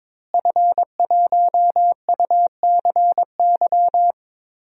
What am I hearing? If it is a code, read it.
Morse code, heard as a keyed tone, about 22 wpm, 700 Hz. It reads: F1UCY
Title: DXCC-100 Call Signs - 5 Characters - Repeated 3x - 22wpm